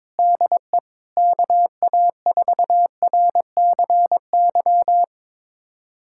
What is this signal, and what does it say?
Morse code, heard as a keyed tone, about 21 wpm, 700 Hz. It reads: DE KA4RCY
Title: USA Callsign DE Callsign 22wpm